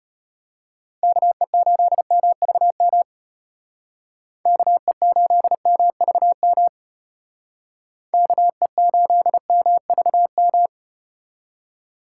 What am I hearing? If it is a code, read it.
Morse code, heard as a keyed tone, about 32 wpm, 700 Hz. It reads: KE8MVM KE8MVM KE8MVM